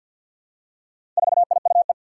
Morse code, heard as a keyed tone, about 51 wpm, 700 Hz. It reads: 4IUE
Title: Four Challenging Letter-Number Combinations 50wpm